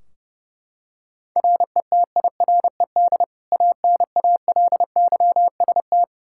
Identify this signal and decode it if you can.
Morse code, heard as a keyed tone, about 29 wpm, 700 Hz. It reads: RETIRED ANALYST
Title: K1USN SST QSO Practice 30wpm